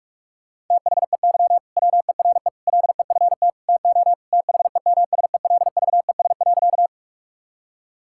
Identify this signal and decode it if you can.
Morse code, heard as a keyed tone, about 46 wpm, 700 Hz. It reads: THEY WERE LEFT TO THEMSELVES.
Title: Sentences from Top 1000 Words - Review Part 5 - 45wpm